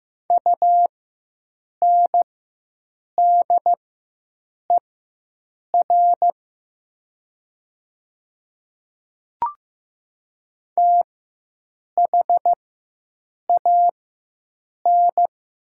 Morse code, heard as a keyed tone, about 15 wpm, spaced as for 7 wpm, 700 Hz. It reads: UNDER THAN
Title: 4x Character Spacing - Top 500 Words - 15wpm